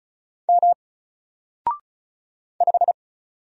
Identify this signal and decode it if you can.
Morse code, heard as a keyed tone, about 36 wpm, 700 Hz. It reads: M 5
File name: Single-Letter-Number-35wpm